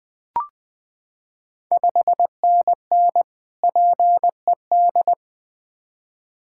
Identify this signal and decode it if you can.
Morse code, heard as a keyed tone, about 20 wpm, 700 Hz. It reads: 5NN PED
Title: Ontario QSO Party Exchange 20wpm